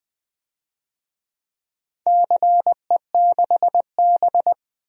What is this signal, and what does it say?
Morse code, heard as a keyed tone, about 20 wpm, 700 Hz. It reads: CE6B
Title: DXCC-100 Call Signs - 4 Characters - Repeated 2x - 20wpm